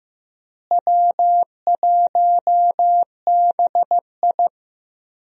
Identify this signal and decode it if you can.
Morse code, heard as a keyed tone, about 15 wpm, 700 Hz. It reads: W1BI
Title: USA Call Signs - 4 Characters - Repeated 3x - 15wpm